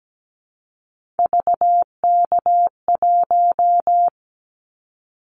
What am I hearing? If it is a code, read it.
Morse code, heard as a keyed tone, about 17 wpm, 700 Hz. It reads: VK1
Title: Callsign Prefixes for Top 50 DXCC - 17wpm